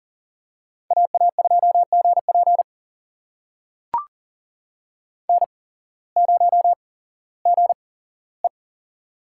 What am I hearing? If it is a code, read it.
Morse code, heard as a keyed tone, about 40 wpm, spaced as for 19 wpm, 690 Hz. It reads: AA2GP N0GE
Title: USA Call Signs - 8x Character Spacing - 40wpm